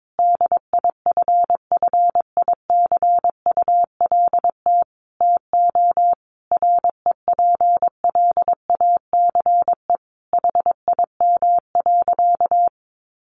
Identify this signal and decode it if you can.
Morse code, heard as a keyed tone, about 22 wpm, 700 Hz. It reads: DIFFICULT TO REPLACE HIM.
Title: Sentences from Top 1400 Words 22wpm